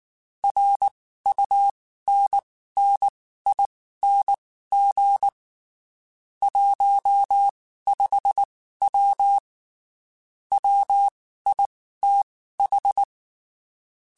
Morse code, heard as a keyed tone, about 19 wpm, spaced as for 14 wpm, 800 Hz. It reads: RUNNING 15W WITH
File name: QSOs / QSOs-mp3s / WD0BGP 1@18